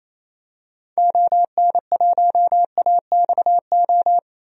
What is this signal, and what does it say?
Morse code, heard as a keyed tone, about 28 wpm, 700 Hz. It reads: ON1AXO
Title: DXCC-100 Call Signs - 6 Characters - 28wpm